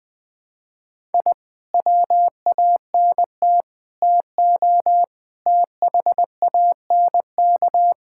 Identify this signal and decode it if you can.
Morse code, heard as a keyed tone, about 20 wpm, 700 Hz. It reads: I WANT TO THANK
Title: Sentences with 9-Word Phrases - 20wpm